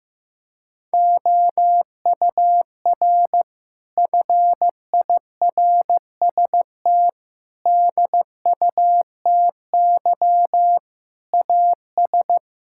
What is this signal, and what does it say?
Morse code, heard as a keyed tone, about 16 wpm, 700 Hz. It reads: OUR FIRST DUTY AS
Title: Sentences from Top 1600 Words 15wpm